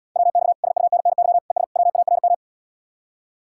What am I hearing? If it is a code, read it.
Morse code, heard as a keyed tone, about 78 wpm, 700 Hz. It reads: WX SUNNY ES WARM